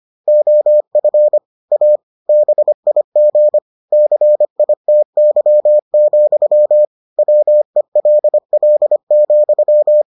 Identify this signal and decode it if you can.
Morse code, heard as a keyed tone, about 25 wpm, 600 Hz. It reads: OF A BIG CITY, WELL,